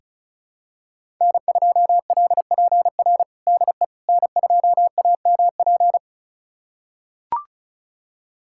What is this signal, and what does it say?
Morse code, heard as a keyed tone, about 36 wpm, 700 Hz. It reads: N2LPR DE N2AMP
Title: USA Callsign DE Callsign - 5 Characters - 35wpm